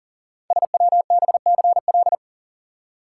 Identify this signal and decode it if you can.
Morse code, heard as a keyed tone, about 40 wpm, 700 Hz. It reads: SWBCL
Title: Code Groups - 5 Letters - 40wpm